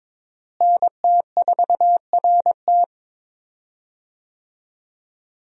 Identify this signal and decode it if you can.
Morse code, heard as a keyed tone, about 22 wpm, 700 Hz. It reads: NT4RT
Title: USA Call Signs - 5 Characters - Repeated 3x - 22wpm